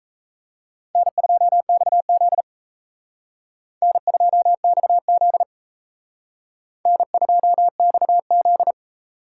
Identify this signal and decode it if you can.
Morse code, heard as a keyed tone, about 34 wpm, 700 Hz. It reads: N2XZ N2XZ N2XZ
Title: Speed Racing - USA 1x2 Call Signs 28wpm